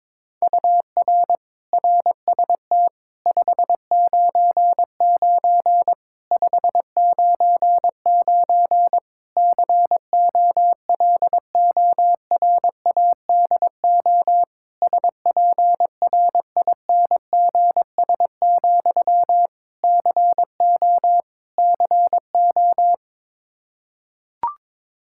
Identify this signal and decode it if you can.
Morse code, heard as a keyed tone, about 22 wpm, 700 Hz. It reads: UR RST 599 599 COLORADO SPRINGS, CO CO
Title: K1USN SST QSO Practice 22wpm